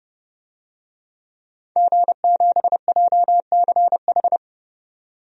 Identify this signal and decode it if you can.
Morse code, heard as a keyed tone, about 31 wpm, 700 Hz. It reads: G7JCH
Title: DXCC-100 Call Signs - 5 Characters - 30wpm